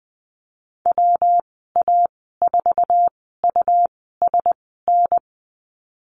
Morse code, heard as a keyed tone, about 20 wpm, spaced as for 15 wpm, 700 Hz. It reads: WA4USN
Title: 167 Lesson 27g - Number 7 - Call Signs 20wpm